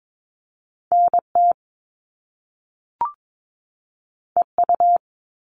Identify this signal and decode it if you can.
Morse code, heard as a keyed tone, about 22 wpm, 700 Hz. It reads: NT EU
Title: Top Two Letter Combinations in Words 22wpm